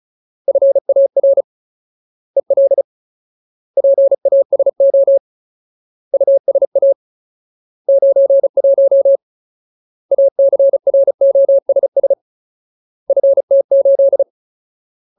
Morse code, heard as a keyed tone, about 34 wpm, 550 Hz. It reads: FAR EL PASO USA 91 ACROSS FT8
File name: Ultimate Word List / 35WPM Ultimate